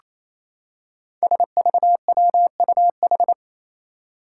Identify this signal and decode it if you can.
Morse code, heard as a keyed tone, about 28 wpm, 700 Hz. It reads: SVWUH
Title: Five Challenging Letter-Number Combinations 28wpm